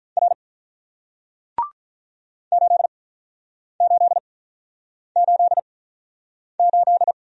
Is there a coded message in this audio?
Yes: Morse code from 0.2 to 7.1 s, heard as a keyed tone, about 48 wpm, spaced as for 9 wpm, 700 Hz. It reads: R 8888